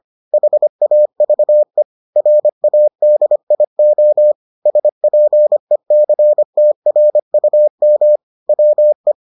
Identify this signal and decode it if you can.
Morse code, heard as a keyed tone, about 25 wpm, 600 Hz. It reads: HAVE RADIO SPECTRUM WE